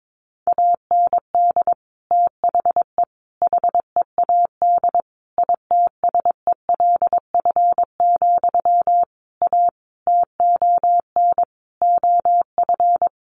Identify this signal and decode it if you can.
Morse code, heard as a keyed tone, about 22 wpm, 700 Hz. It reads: AND THE HEAD ITSELF, A TON OF
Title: A Sound Of Thunder 22wpm